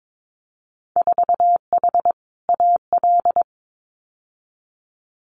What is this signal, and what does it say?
Morse code, heard as a keyed tone, about 22 wpm, 700 Hz. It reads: 4H AL
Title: Winter Field Day Exchanges Encore 22wpm